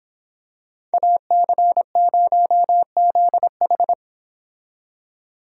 Morse code, heard as a keyed tone, about 26 wpm, 700 Hz. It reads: AC0ZH